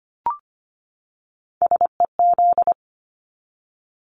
Morse code, heard as a keyed tone, about 25 wpm, 700 Hz. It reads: SEZ
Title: Abbreviations 25wpm